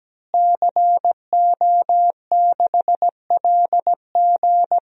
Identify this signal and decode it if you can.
Morse code, heard as a keyed tone, about 17 wpm, 700 Hz. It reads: CO6LG
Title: DXCC-100 Call Signs - 5 Characters - Repeated 3x - 17wpm